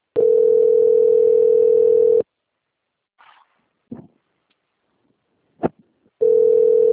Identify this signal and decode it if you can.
Ringback tone (North America), 440 + 480 Hz.